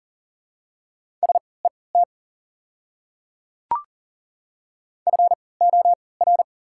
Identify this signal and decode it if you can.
Morse code, heard as a keyed tone, about 40 wpm, spaced as for 23 wpm, 690 Hz. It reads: SET FOR
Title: 3x Character Spacing - Top 200 Words - 40wpm